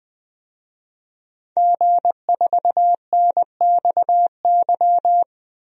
Morse code, heard as a keyed tone, about 20 wpm, 700 Hz. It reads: G4NXY
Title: DXCC-100 Call Signs - 5 Characters - 20wpm